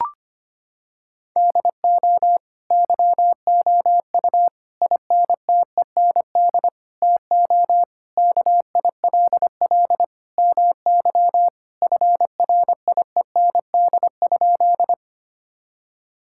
Morse code, heard as a keyed tone, about 25 wpm, 700 Hz. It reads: DO YOU INTEND TO KILL MY FRIEND?
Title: Sentences from Top 1600 Words 25wpm